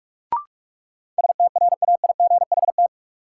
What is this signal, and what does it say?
Morse code, heard as a keyed tone, about 46 wpm, 700 Hz. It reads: STRAIGHT
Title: Sentences from Top 1000 Words - New Word Drill - 45wpm